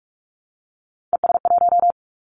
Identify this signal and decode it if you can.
Morse code, heard as a keyed tone, about 46 wpm, 700 Hz. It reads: ES1
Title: Callsign Prefixes for Top 100 DXCC - 45wpm